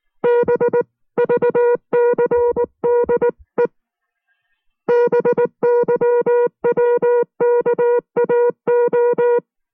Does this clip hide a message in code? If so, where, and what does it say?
Yes: Morse code from 0.2 to 9.4 s, heard as a keyed tone, about 19 wpm, 480 Hz. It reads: BVCDE BYWKAO